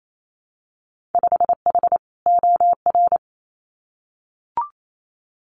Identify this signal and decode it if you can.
Morse code, heard as a keyed tone, about 28 wpm, 700 Hz. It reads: 5H OR
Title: Winter Field Day Exchanges Encore 28wpm